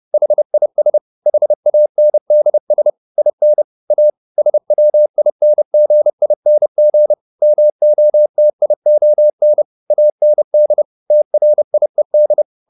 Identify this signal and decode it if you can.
Morse code, heard as a keyed tone, about 30 wpm, 600 Hz. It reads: HIS HANDS IN A SWINGING MOTION AND TRIED